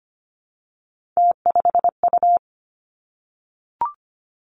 Morse code, heard as a keyed tone, about 25 wpm, 700 Hz. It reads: T5U
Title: Three Challenging Letter-Number Combinations 25wpm